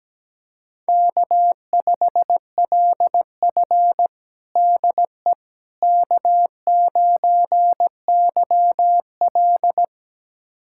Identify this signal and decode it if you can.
Morse code, heard as a keyed tone, about 17 wpm, 700 Hz. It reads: K5LF DE K9YL
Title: USA Callsign DE Callsign - 4 Characters - 17wpm